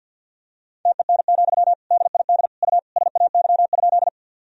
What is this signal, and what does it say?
Morse code, heard as a keyed tone, about 52 wpm, 700 Hz. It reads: TEN, DID U SAY?
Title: Sentences from Top 400 Words and Abbreviations 50wpm